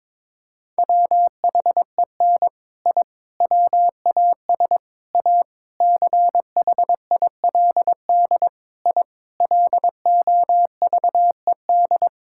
Morse code, heard as a keyed tone, about 21 wpm, 700 Hz. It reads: WHEN I WAS A CHILD I LOVED
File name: practice / 22wpm / Sentences with 3-Word Phrases - 22wpm